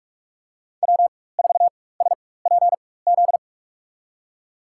Morse code, heard as a keyed tone, about 46 wpm, spaced as for 21 wpm, 700 Hz. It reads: W4SPZ
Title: USA Call Signs - 4x Character Spacing - 45wpm